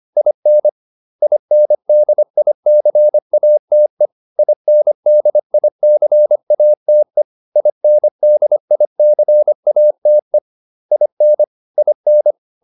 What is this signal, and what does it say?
Morse code, heard as a keyed tone, about 25 wpm, 600 Hz. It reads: IN INDICATE INDICATE INDICATE IN IN